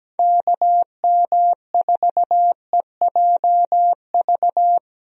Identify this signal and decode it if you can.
Morse code, heard as a keyed tone, about 17 wpm, 700 Hz. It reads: KM4EJV
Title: USA Call Signs - 6 Characters - Repeated 2x - 17wpm